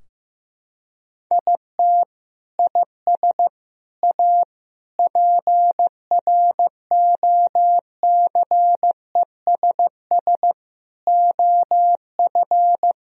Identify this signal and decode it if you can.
Morse code, heard as a keyed tone, about 14 wpm, 700 Hz. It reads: IT IS A PROCESS OF
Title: Sentences from Top 1000 Words - Review Part 4 - 15wpm